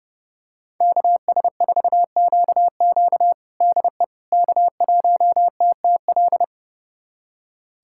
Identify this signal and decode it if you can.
Morse code, heard as a keyed tone, about 30 wpm, 700 Hz. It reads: KS4QQ DE K1TTL
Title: USA Callsign DE Callsign - 5 Characters - 30wpm